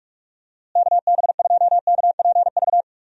Morse code, heard as a keyed tone, about 46 wpm, 700 Hz. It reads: KB2KPV
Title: USA Call Signs - 6 Characters - Repeated 3x - 45wpm